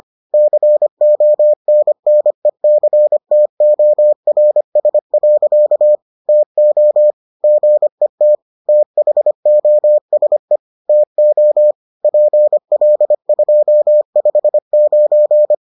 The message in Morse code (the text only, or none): CONNECTORS. TO GET THOSE TO PL259